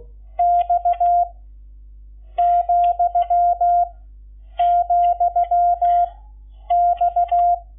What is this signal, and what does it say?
Morse code, heard as a keyed tone, about 16 wpm, 700 Hz. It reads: X , , X
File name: Lecciones / CLASE Nº11